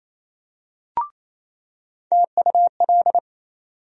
Morse code, heard as a keyed tone, about 28 wpm, 700 Hz. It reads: TUL
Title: USA Call Sign Postfixes - 3 Letters 28wpm